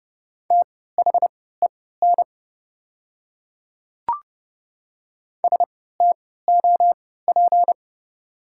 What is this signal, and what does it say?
Morse code, heard as a keyed tone, about 30 wpm, spaced as for 17 wpm, 700 Hz. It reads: THEN STOP
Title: 3x Character Spacing - Top 500 Words - 30wpm